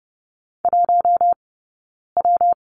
Morse code, heard as a keyed tone, about 30 wpm, 700 Hz. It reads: 1 W
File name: Lesson 19d - Letter w - Two Characters 30wpm